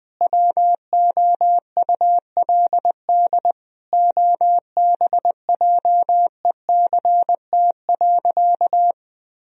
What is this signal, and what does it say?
Morse code, heard as a keyed tone, about 21 wpm, 700 Hz. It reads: WOULD OBJECT.
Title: Sentences from Top 1000 Words - Review Part 3 - 20wpm